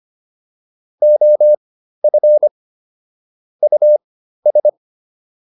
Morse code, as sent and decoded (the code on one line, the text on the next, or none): --- ..-. / ..- ...
OF US